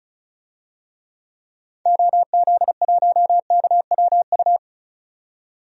Morse code, heard as a keyed tone, about 36 wpm, 700 Hz. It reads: OZ1KWU
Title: DXCC-100 Call Signs - 6 Characters - 35wpm